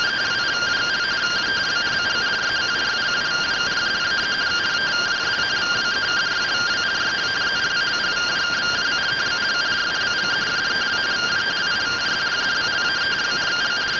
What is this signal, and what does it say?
Two-tone teleprinter signal, about 170 Hz shift.